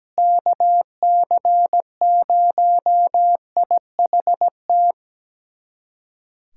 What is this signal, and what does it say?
Morse code, heard as a keyed tone, about 17 wpm, 700 Hz. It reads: KC0IHT
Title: USA Callsign DE Callsign - 6 Characters - 17wpm